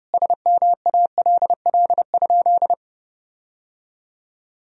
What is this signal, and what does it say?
Morse code, heard as a keyed tone, about 32 wpm, 700 Hz. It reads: SMALL?
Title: Sentences from Top 1000 Words - Review Part 1 - 30wpm